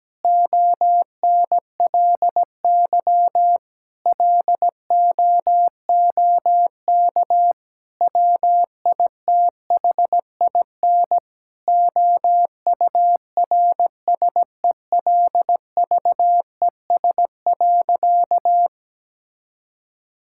Morse code, heard as a keyed tone, about 17 wpm, 700 Hz. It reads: ONLY LOOK WITHIN OURSELVES.
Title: Sentences from Top 1600 Words 17wpm